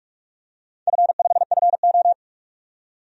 Morse code, heard as a keyed tone, about 45 wpm, 700 Hz. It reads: F5FO
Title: DXCC-100 Call Signs - 4 Characters - 45wpm